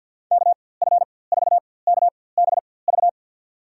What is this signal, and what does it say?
Morse code, heard as a keyed tone, about 53 wpm, spaced as for 24 wpm, 700 Hz. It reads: KF4KBV
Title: USA Call Signs - 6 Characters - 4x Character Spacing - 50wpm